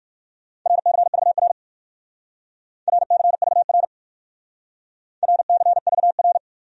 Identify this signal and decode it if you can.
Morse code, heard as a keyed tone, about 57 wpm, 700 Hz. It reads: RCVR RCVR RCVR